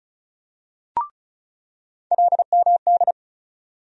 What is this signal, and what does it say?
Morse code, heard as a keyed tone, about 36 wpm, 700 Hz. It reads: LMD
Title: USA Call Sign Postfixes - 3 Letters 35wpm